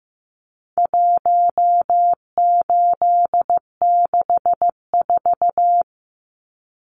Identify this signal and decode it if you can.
Morse code, heard as a keyed tone, about 15 wpm, 700 Hz. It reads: 1864
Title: Numbers - 4-Digit 15wpm